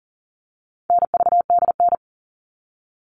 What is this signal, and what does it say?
Morse code, heard as a keyed tone, about 40 wpm, 700 Hz. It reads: NVDN
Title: All Cut Numbers - 4-Digit 40wpm